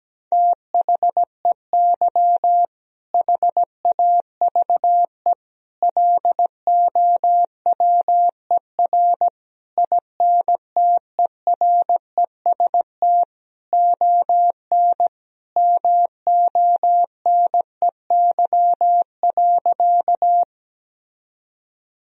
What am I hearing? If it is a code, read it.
Morse code, heard as a keyed tone, about 17 wpm, 700 Hz. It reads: THEY HAVE LOWER INTEREST ON MONEY.
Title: Sentences from Top 1000 Words - Review Part 4 - 17wpm